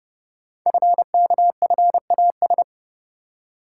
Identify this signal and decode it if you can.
Morse code, heard as a keyed tone, about 30 wpm, 700 Hz. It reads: FKFAS
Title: Code Groups - 5 Letter:Numbers - 30wpm